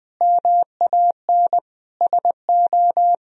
Morse code, heard as a keyed tone, about 20 wpm, 700 Hz. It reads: MAN SO
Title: Sentences from Top 1000 Words - Review Part 1 - 20wpm